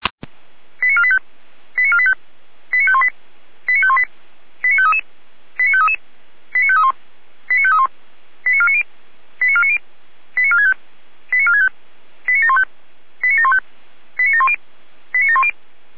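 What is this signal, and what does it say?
Five-tone selective call (ZVEI), heard as quick runs of five tones, tones of about 70 ms. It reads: 89485 89485 89518 89518 89533 89533 89531 89531 89490 89490 89465 89465 89715 89715 89710 89710
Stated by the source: ZVEI five-tone selective call received on an SDR, logged by the fire brigade Feuerwehr Orlamünde.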